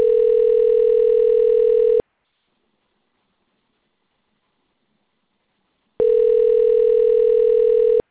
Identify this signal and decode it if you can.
Ringback tone (North America), 440 + 480 Hz.